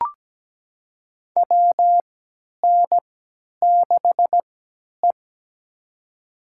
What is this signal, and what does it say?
Morse code, heard as a keyed tone, about 17 wpm, spaced as for 10 wpm, 700 Hz. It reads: WN6E
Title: USA Call Signs - 4 Characters - 3x Character Spacing - 17wpm